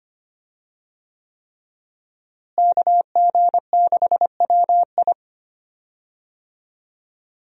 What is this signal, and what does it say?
Morse code, heard as a keyed tone, about 25 wpm, 700 Hz. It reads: KG6WI
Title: USA Call Signs - 5 Characters - Repeated 3x - 25wpm